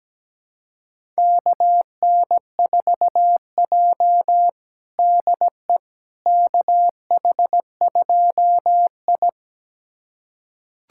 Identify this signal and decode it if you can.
Morse code, heard as a keyed tone, about 17 wpm, 700 Hz. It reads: KN4J DE KH2I
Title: USA Callsign DE Callsign - 4 Characters - 17wpm